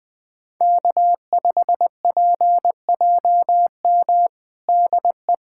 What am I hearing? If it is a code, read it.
Morse code, heard as a keyed tone, about 20 wpm, 700 Hz. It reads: K5PJM DE
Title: USA Callsign DE Callsign - 5 Characters - 20wpm